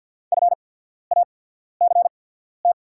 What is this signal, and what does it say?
Morse code, heard as a keyed tone, about 48 wpm, spaced as for 13 wpm, 690 Hz. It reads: FACT
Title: 8x Character Spacing - Top 500 Words - 50wpm